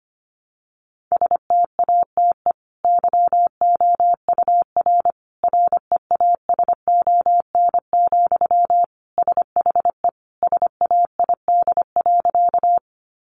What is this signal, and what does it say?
Morse code, heard as a keyed tone, about 25 wpm, 700 Hz. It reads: STATE YOUR REASON, SHE SAID.
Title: Sentences from Top 700 Words 25wpm